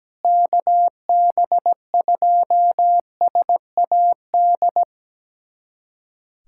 Morse code, heard as a keyed tone, about 17 wpm, 700 Hz. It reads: KB2SAD
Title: USA Callsign DE Callsign 17wpm